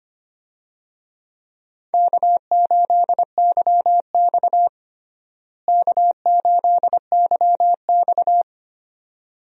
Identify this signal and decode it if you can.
Morse code, heard as a keyed tone, about 25 wpm, 700 Hz. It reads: K8YX K8YX
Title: USA Call Signs - 4 Characters - Repeated 2x - 25wpm